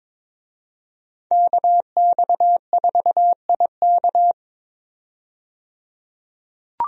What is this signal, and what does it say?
Morse code, heard as a keyed tone, about 22 wpm, 700 Hz. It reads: KX4IK
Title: USA Call Signs - 5 Characters - 22wpm